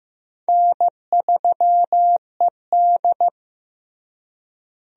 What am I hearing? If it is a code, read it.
Morse code, heard as a keyed tone, about 14 wpm, 700 Hz. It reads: N3ED
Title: USA Call Signs - 4 Characters - Repeated 3x - 15wpm